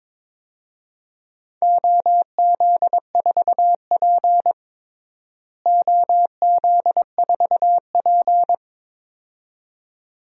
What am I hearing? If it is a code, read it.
Morse code, heard as a keyed tone, about 22 wpm, 700 Hz. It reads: OZ4P OZ4P
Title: DXCC-100 Call Signs - 4 Characters - Repeated 2x - 22wpm